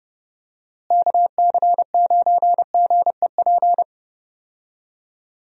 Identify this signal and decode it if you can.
Morse code, heard as a keyed tone, about 31 wpm, 700 Hz. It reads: KC9GEP
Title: USA Call Signs - 6 Characters - Repeated 2x - 30wpm